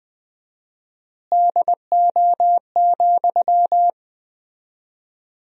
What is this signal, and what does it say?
Morse code, heard as a keyed tone, about 20 wpm, 700 Hz. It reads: DO,
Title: 4x Word-Spacing - Sentences from Top 100 Words - 20wpm